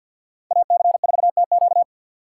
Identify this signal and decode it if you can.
Morse code, heard as a keyed tone, about 49 wpm, 700 Hz. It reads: AC4TQ
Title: USA Call Signs - 4x Character Spacing - 50wpm